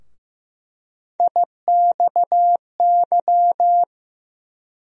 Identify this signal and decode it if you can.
Morse code, heard as a keyed tone, about 15 wpm, 700 Hz. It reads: IXY
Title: USA Call Sign Postfixes - 3 Letters 15wpm